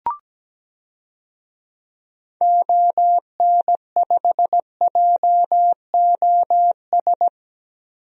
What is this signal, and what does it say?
Morse code, heard as a keyed tone, about 17 wpm, 700 Hz. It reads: ON5JOS
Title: DXCC-100 Call Signs - 6 Characters - Repeated 2x - 17wpm